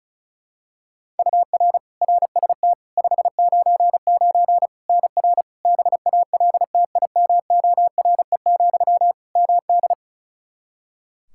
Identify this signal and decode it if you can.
Morse code, heard as a keyed tone, about 34 wpm, 700 Hz. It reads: UR RST 599 NR BALTIMORE, MD
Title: Practice QSO - Exchange 1 - Beginning - 35wpm